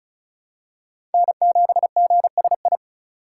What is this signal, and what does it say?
Morse code, heard as a keyed tone, about 34 wpm, 700 Hz. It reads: N7GSI
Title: USA Call Signs - 5 Characters - 2x Character Spacing - 35wpm